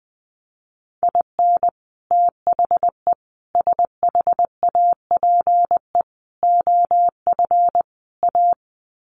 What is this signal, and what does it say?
Morse code, heard as a keyed tone, about 20 wpm, 700 Hz. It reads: IN THE SHAPE OF A
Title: Sets of 5 Words Encore 20wpm